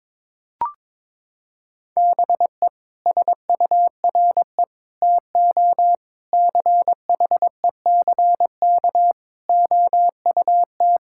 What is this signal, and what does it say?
Morse code, heard as a keyed tone, about 22 wpm, 700 Hz. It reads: BE SURE TO CHECK OUT
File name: practice / 22wpm / Sets of 6 Words - 22wpm